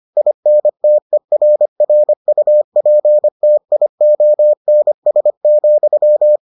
Morse code, heard as a keyed tone, about 25 wpm, 600 Hz. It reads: INTERRUPTIONS,